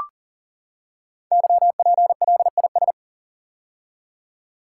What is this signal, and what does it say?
Morse code, heard as a keyed tone, about 42 wpm, 690 Hz. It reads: YPLIS
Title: Code Groups - 5 Letters - 40wpm